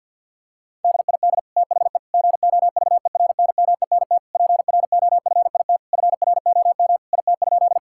Morse code, heard as a keyed tone, about 52 wpm, 700 Hz. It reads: DID THE GOVERNMENT PROFIT FROM IT?
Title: Sentences from Top 1000 Words - Review Part 5 - 50wpm